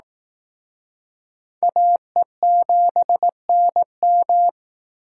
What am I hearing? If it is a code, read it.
Morse code, heard as a keyed tone, about 18 wpm, 700 Hz. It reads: AE7NM
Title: Speed Racing - USA Call Signs 15wpm